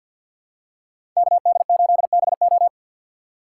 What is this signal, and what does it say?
Morse code, heard as a keyed tone, about 52 wpm, 700 Hz. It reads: KD8BO
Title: USA Call Signs - 2x Character Spacing - 50wpm